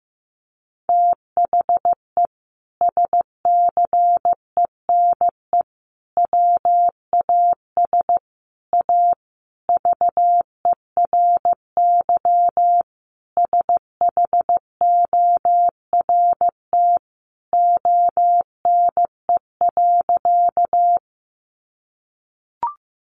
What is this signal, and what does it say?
Morse code, heard as a keyed tone, about 15 wpm, 700 Hz. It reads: THE SCENE WAS A VERY SHORT ONE.
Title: Sentences from Top 1100 Words 15wpm